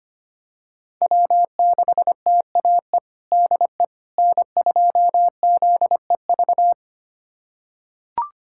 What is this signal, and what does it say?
Morse code, heard as a keyed tone, about 25 wpm, 700 Hz. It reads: W6TAE DE N2ZEV
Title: USA Callsign DE Callsign - 5 Characters - 25wpm